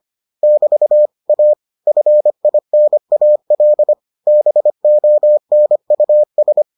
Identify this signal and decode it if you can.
Morse code, heard as a keyed tone, about 25 wpm, 600 Hz. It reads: <BT> A FINAL BONUS